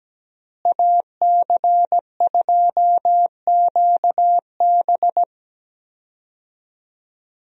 Morse code, heard as a keyed tone, about 17 wpm, 700 Hz. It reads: AC2QB
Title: USA Call Signs - 5 Characters - Repeated 2x - 17wpm